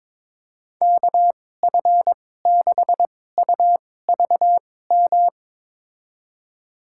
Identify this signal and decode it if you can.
Morse code, heard as a keyed tone, about 22 wpm, spaced as for 16 wpm, 700 Hz. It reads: KF6UVM
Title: USA Call Signs - 2x Character Spacing - 22wpm